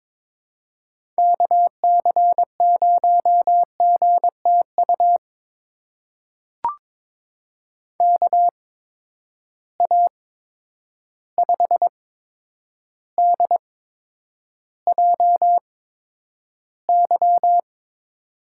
Morse code, heard as a keyed tone, about 22 wpm, spaced as for 9 wpm, 700 Hz. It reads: KC0GTU KA5DJY
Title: USA Call Signs - 6 Characters - 8x Character Spacing - 22wpm